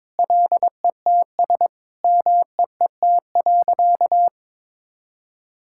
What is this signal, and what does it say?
Morse code, heard as a keyed tone, about 22 wpm, 700 Hz. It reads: LETS MEET.
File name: practice / 22wpm / Sentences with 3-Word Phrases - 22wpm